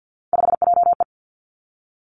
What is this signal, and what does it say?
Morse code, heard as a keyed tone, about 50 wpm, 700 Hz. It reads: 5JE
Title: Three Challenging Letter-Number Combinations 50wpm